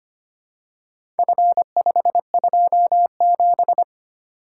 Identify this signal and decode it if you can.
Morse code, heard as a keyed tone, about 25 wpm, 700 Hz. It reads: F527
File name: practice / 25wpm / Four Letter-Number Combinations - 25wpm